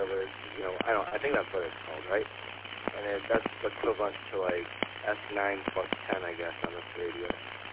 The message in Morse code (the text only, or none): none